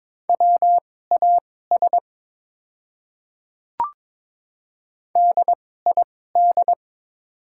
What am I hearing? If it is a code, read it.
Morse code, heard as a keyed tone, about 22 wpm, spaced as for 16 wpm, 700 Hz. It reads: WAS DID
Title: 2x Character Spacing - Top 200 Words - 22wpm